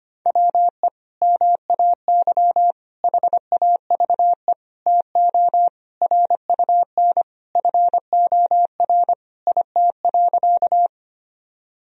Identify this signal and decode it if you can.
Morse code, heard as a keyed tone, about 25 wpm, 700 Hz. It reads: WE MAY HAVE TO RUN FOR IT.
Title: Sentences from Top 1000 Words - Review Part 1 - 25wpm